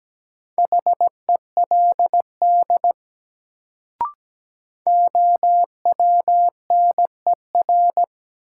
Morse code, heard as a keyed tone, about 17 wpm, 700 Hz. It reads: HELD OWNER
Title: Sentences from Top 1600 Words - New Word Drill Alt - 17wpm